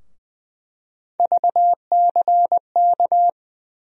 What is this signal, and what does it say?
Morse code, heard as a keyed tone, about 20 wpm, 700 Hz. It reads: VCK
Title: USA Call Sign Postfixes - 3 Letters 20wpm